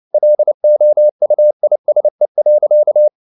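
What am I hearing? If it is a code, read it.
Morse code, heard as a keyed tone, about 29 wpm, 600 Hz. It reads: LOUISE.
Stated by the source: Fast (QRQ) Morse at 29 wpm, from CW med Gnister.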